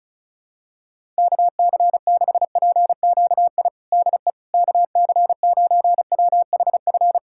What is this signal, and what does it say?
Morse code, heard as a keyed tone, about 34 wpm, 700 Hz. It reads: KC6PQI DE KC9WHF
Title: USA Callsign DE Callsign - 6 Characters - 35wpm